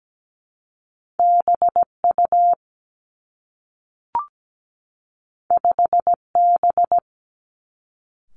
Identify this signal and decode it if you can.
Morse code, heard as a keyed tone, about 17 wpm, 700 Hz. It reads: BU 5B
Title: Two Challenging Letter-Number Combinations 17wpm